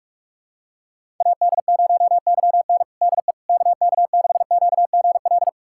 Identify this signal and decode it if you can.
Morse code, heard as a keyed tone, about 43 wpm, 700 Hz. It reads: AD0YN DE KK6QGL
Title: USA Callsign DE Callsign 45wpm